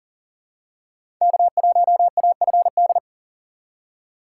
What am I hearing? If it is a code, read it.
Morse code, heard as a keyed tone, about 40 wpm, 700 Hz. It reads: K1AFD
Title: USA Call Signs - 5 Characters - 2x Character Spacing - 40wpm